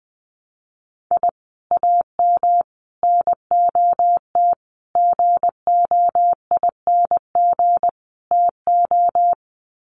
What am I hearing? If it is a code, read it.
Morse code, heard as a keyed tone, about 20 wpm, 700 Hz. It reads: I AM NOT GOING TO
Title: Sets of 5 Words Encore 20wpm